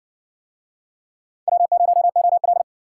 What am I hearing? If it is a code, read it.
Morse code, heard as a keyed tone, about 53 wpm, 700 Hz. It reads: W0OL